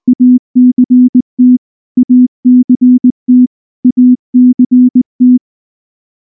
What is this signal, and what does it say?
Morse code, heard as a keyed tone, about 19 wpm, 250 Hz. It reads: ACT ACT ACT
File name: morse / morse-training-20WPM-3